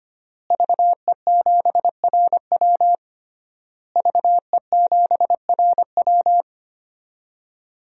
Morse code, heard as a keyed tone, about 25 wpm, 700 Hz. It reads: VE7RW VE7RW
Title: DXCC-100 Call Signs - 5 Characters - Repeated 3x - 25wpm